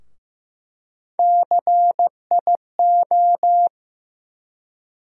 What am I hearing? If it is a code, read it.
Morse code, heard as a keyed tone, about 15 wpm, 700 Hz. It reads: CIO
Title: USA Call Sign Postfixes - 3 Letters 15wpm